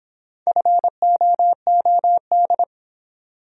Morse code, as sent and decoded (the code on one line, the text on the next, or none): ..-. --- --- -..
FOOD